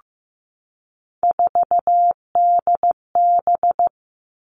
Morse code, heard as a keyed tone, about 15 wpm, 700 Hz. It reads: 4DB
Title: All Cut Numbers - 3-Digit 15wpm